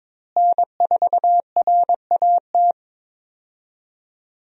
Morse code, heard as a keyed tone, about 22 wpm, 700 Hz. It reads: N4RAT
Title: USA Call Signs - 5 Characters - Repeated 2x - 22wpm